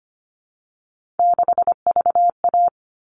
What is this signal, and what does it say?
Morse code, heard as a keyed tone, about 25 wpm, 700 Hz. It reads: 6VA
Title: All Cut Numbers - 3-Digit 25wpm